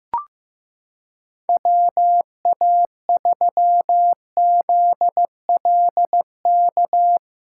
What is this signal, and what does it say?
Morse code, heard as a keyed tone, about 15 wpm, 700 Hz. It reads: WA3ZLK
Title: USA Callsign DE Callsign 15wpm